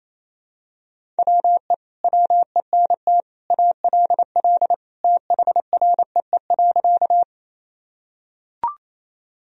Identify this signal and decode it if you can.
Morse code, heard as a keyed tone, about 28 wpm, 700 Hz. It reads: WE WENT ALL THREE.
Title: Sentences from Top 1000 Words - Review Part 1 - 28wpm